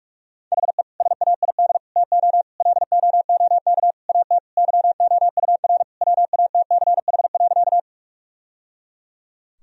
Morse code, heard as a keyed tone, about 46 wpm, 700 Hz. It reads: HE SAID TO LOOK AT YOUR WATCH.
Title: Sentences from Top 1000 Words - Review Part 1 - 45wpm